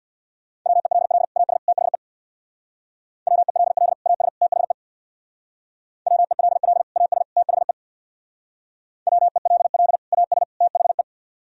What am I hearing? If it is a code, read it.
Morse code, heard as a keyed tone, about 60 wpm, 700 Hz. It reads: WELL AS THE WELL AS THE WELL AS THE WELL AS THE
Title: Speed-Racing - Sets of 3 Words 50wpm